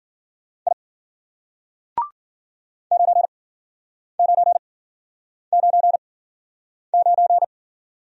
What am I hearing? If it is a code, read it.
Morse code, heard as a keyed tone, about 56 wpm, spaced as for 9 wpm, 700 Hz. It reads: I 9999